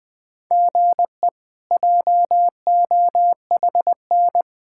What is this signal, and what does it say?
Morse code, heard as a keyed tone, about 21 wpm, 700 Hz. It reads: GE JOHN
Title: K1USN SST QSO Practice - 20wpm